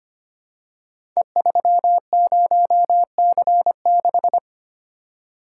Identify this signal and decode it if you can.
Morse code, heard as a keyed tone, about 26 wpm, 700 Hz. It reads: E30C6
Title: Code Groups - 5 Letter:Numbers - 25wpm